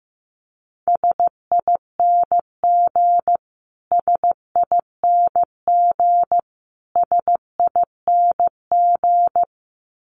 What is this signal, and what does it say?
Morse code, heard as a keyed tone, about 15 wpm, 700 Hz. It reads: SING SING SING
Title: Top 500 Words Repeated 15wpm